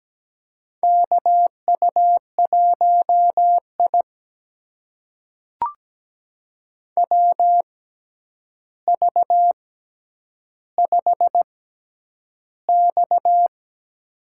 Morse code, heard as a keyed tone, about 17 wpm, spaced as for 9 wpm, 700 Hz. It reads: KU1I WV5X
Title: USA Call Signs - 6x Character Spacing - 17wpm